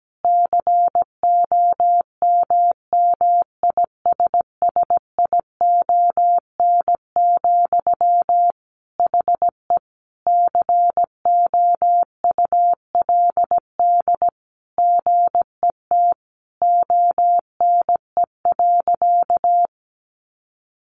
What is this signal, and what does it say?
Morse code, heard as a keyed tone, about 17 wpm, 700 Hz. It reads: COMMISSION, HE COULD GET ONE.
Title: Sentences from Top 1400 Words 17wpm